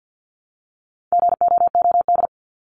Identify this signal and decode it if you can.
Morse code, heard as a keyed tone, about 50 wpm, 700 Hz. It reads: GOOD
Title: ES:S Suffix - Top 100 Words with Base Words 50wpm